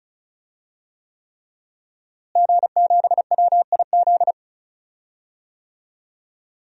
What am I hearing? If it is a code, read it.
Morse code, heard as a keyed tone, about 36 wpm, 700 Hz. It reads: G7WIZ
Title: DXCC-100 Call Signs - 5 Characters - Repeated 3x - 35wpm